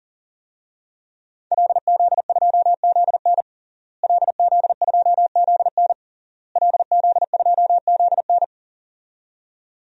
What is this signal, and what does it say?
Morse code, heard as a keyed tone, about 40 wpm, 700 Hz. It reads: LZ2ZN LZ2ZN LZ2ZN
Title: DXCC-100 Call Signs - 5 Characters - Repeated 3x - 40wpm